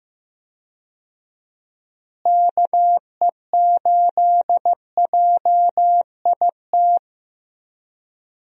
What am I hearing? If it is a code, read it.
Morse code, heard as a keyed tone, about 15 wpm, 700 Hz. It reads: KE8JIT
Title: USA Call Signs - 6 Characters - Repeated 2x - 15wpm